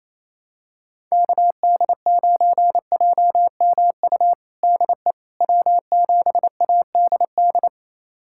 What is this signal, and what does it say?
Morse code, heard as a keyed tone, about 28 wpm, 700 Hz. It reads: KD9JMU DE W7ADD
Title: USA Callsign DE Callsign 28wpm